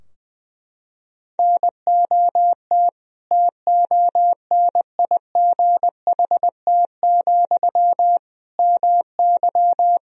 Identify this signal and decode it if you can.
Morse code, heard as a keyed tone, about 20 wpm, 700 Hz. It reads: NOT TONIGHT, MY
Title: Sentences from Top 1000 Words - Review Part 3 - 20wpm